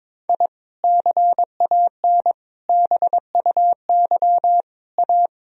Morse code, heard as a keyed tone, about 22 wpm, 700 Hz. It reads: I CAN BUY A
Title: Sentences from Top 1000 Words - Review Part 1 - 22wpm